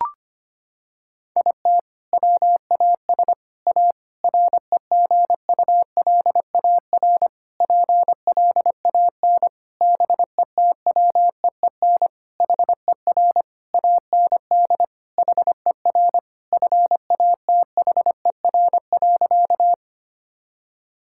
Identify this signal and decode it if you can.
Morse code, heard as a keyed tone, about 25 wpm, 700 Hz. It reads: IT WAS A REGULAR PLAN BETWEEN HER AND HER FATHER.
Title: Sentences from Top 1000 Words - Review Part 6 - 25wpm